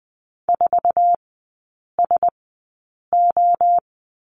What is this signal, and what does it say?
Morse code, heard as a keyed tone, about 20 wpm, spaced as for 8 wpm, 700 Hz. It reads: 4SO
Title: 047 Lesson 9e - Number 4 - Three Characters 20wpm